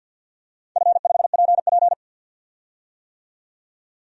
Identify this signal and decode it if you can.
Morse code, heard as a keyed tone, about 50 wpm, 700 Hz. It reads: F5PP
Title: DXCC-100 Call Signs - 4 Characters - 50wpm